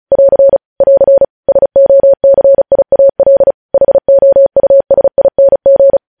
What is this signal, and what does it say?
Morse code, heard as a keyed tone, about 36 wpm, 550 Hz. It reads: <AR> <AR> SOCIAL HOUSING
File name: podcast / News headlines-1x-35x35